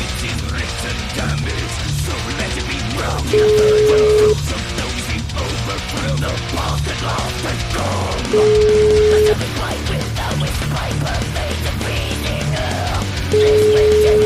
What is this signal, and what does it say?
Ringback tone (Europe, 425 Hz), with noise.